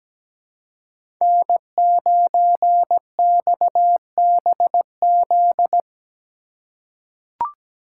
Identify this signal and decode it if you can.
Morse code, heard as a keyed tone, about 17 wpm, 700 Hz. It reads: N9XBZ
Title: USA Call Signs - 5 Characters - 4x Character Spacing - 17wpm